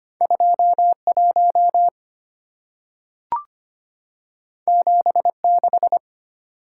Morse code, heard as a keyed tone, about 25 wpm, 690 Hz. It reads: 21 76
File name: Two Letter-Number Combinations - 25wpm